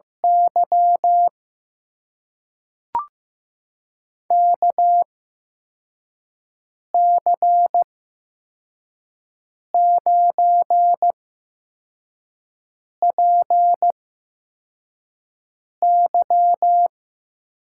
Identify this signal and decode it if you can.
Morse code, heard as a keyed tone, about 15 wpm, spaced as for 4 wpm, 700 Hz. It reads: Y KC9PY
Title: USA Call Signs - 6 Characters - 8x Character Spacing - 15wpm